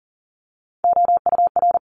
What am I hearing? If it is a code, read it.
Morse code, heard as a keyed tone, about 40 wpm, 700 Hz. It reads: OUR
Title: Top 200 Words Repeated 40wpm